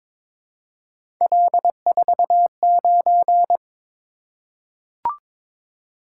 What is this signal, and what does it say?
Morse code, heard as a keyed tone, about 21 wpm, 700 Hz. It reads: L49
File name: practice / 22wpm / Three Letter-Number Combinations - 22wpm